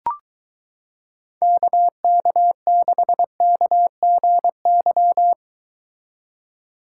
Morse code, heard as a keyed tone, about 22 wpm, 700 Hz. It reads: KK6KGY